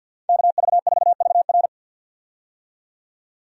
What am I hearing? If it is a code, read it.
Morse code, heard as a keyed tone, about 50 wpm, 700 Hz. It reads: KV4VR
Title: USA Call Signs - 5 Characters - 50wpm